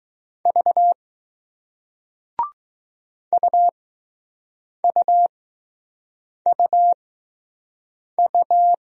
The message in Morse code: V UUUU

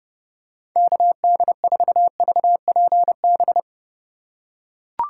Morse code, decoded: KD4VPB

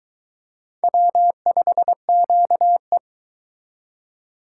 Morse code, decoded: W5QE

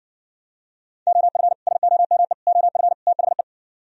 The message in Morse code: OF SOME OF THE

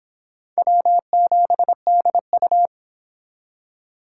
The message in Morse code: W7DU

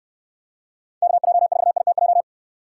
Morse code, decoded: KJ4ITJ

